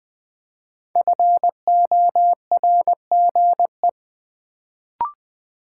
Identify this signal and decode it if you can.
Morse code, heard as a keyed tone, about 20 wpm, 700 Hz. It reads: FORGE
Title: Build A Word - 1 Letter At A Time - 5-Letter Words - 20wpm